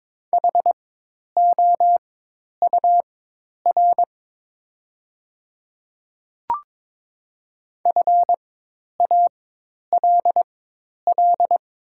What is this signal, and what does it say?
Morse code, heard as a keyed tone, about 22 wpm, spaced as for 10 wpm, 700 Hz. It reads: HOUR FALL
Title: 4x Character Spacing - Top 500 Words - 22wpm